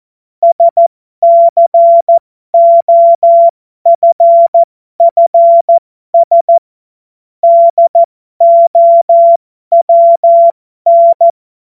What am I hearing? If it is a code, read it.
Morse code, heard as a keyed tone, about 14 wpm, spaced as for 12 wpm, 680 Hz. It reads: SCOFFS DOWN
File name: Morsekurs / Witze